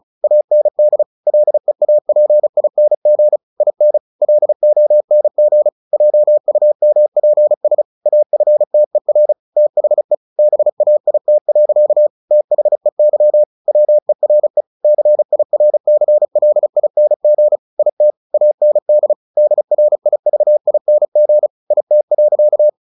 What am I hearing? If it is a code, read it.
Morse code, heard as a keyed tone, about 34 wpm, 600 Hz. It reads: AND LEAPING IN LONG JUMPS AFTER THE BAIT. THEY WERE CIRCLING IT AND DRIVING IT.
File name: the old man and the sea / the old man and the sea 1 35wpm